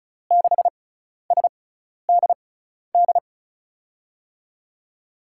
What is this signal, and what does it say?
Morse code, heard as a keyed tone, about 36 wpm, spaced as for 12 wpm, 700 Hz. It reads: 6SDD
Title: USA Call Signs - 6 Characters - 6x Character Spacing - 35wpm